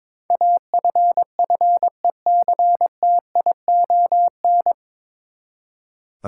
Morse code, heard as a keyed tone, about 21 wpm, 700 Hz. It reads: AFFECTION
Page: Morse Code Ninja practice 22wpm